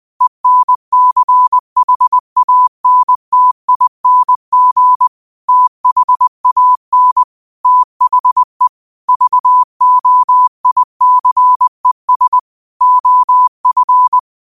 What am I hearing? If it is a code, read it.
Morse code, heard as a keyed tone, about 20 wpm, 1000 Hz. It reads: ENCHANTING THAN THE VOICES OF